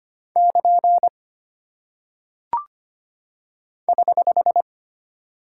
Morse code, heard as a keyed tone, about 25 wpm, 700 Hz. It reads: <KN> <HH>
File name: Single Punctuation-Prosign - 25wpm